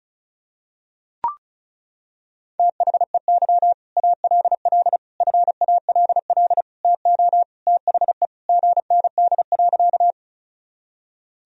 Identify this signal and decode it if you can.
Morse code, heard as a keyed tone, about 34 wpm, 700 Hz. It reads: THEY ALL FALL TO THE GND.
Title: Sentences from Top 500 Words and Abbreviations 35wpm